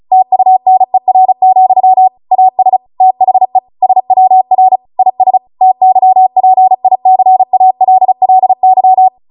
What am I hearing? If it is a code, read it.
Morse code, heard as a keyed tone, about 36 wpm, 750 Hz. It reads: TUNER, AS THE SWR IS TYPICALLY